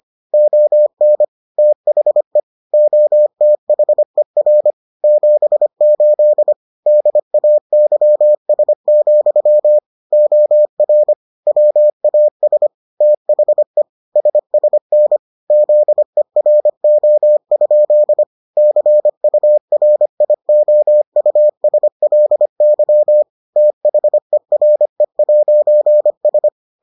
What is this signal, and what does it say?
Morse code, heard as a keyed tone, about 25 wpm, 600 Hz. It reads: ON THE OTHER 78 DAYS, OR WAS THE SSN ZERO? CURIOUSLY THERE'S